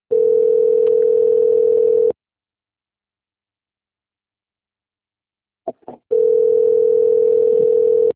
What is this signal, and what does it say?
Ringback tone (North America), 440 + 480 Hz.
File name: out- - - - - .90332